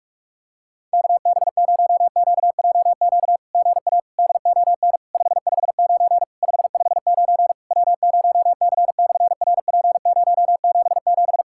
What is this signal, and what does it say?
Morse code, heard as a keyed tone, about 46 wpm, 700 Hz. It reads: KB0QJQ GA DON 559 559 W0C/RP077
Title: SOTA QSO Practice 45wpm